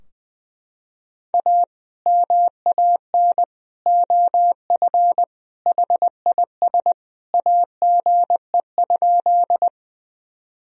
Morse code, heard as a keyed tone, about 21 wpm, 700 Hz. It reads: A MAN OF HIS AGE?
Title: Sentences from Top 1000 Words - Review Part 3 - 20wpm